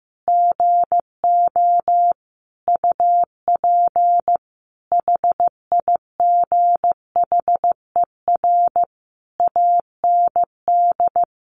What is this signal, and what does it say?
Morse code, heard as a keyed tone, about 15 wpm, 700 Hz. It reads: GO UP HIGHER AND
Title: Additional Easy Phrases 15wpm